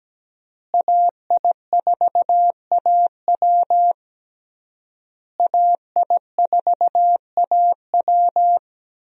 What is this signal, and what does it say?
Morse code, heard as a keyed tone, about 17 wpm, 700 Hz. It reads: AI4AW AI4AW
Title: USA Call Signs - 5 Characters - Repeated 3x - 17wpm